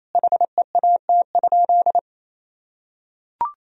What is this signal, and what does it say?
Morse code, heard as a keyed tone, about 28 wpm, 700 Hz. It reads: HEAT?
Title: Sentences from Top 1000 Words - Review Part 3 - 28wpm